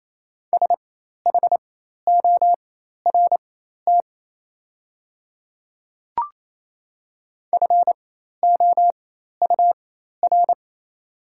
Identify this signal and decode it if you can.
Morse code, heard as a keyed tone, about 27 wpm, spaced as for 13 wpm, 700 Hz. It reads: SHORT FOUR
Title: 4x Character Spacing - Top 500 Words - 28wpm